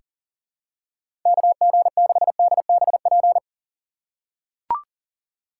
Morse code, heard as a keyed tone, about 40 wpm, 700 Hz. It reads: KG6DBP
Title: USA Call Signs - 6 Characters - 3x Character Spacing - 40wpm